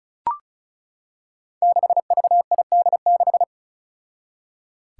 Morse code, heard as a keyed tone, about 36 wpm, 700 Hz. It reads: 6VID6
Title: Five Challenging Letter-Number Combinations 35wpm